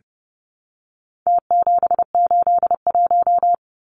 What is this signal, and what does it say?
Morse code, heard as a keyed tone, about 30 wpm, 700 Hz. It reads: T781
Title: Numbers - Contest Style - 4-Digit 30wpm